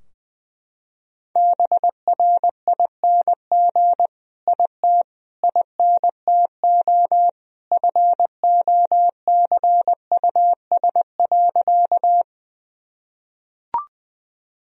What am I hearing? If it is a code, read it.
Morse code, heard as a keyed tone, about 20 wpm, 700 Hz. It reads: BRING IT INTO FOCUS.
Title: Sentences from Top 1000 Words - Review Part 4 - 20wpm